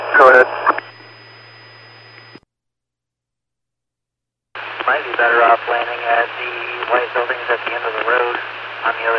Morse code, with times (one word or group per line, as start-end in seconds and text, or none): none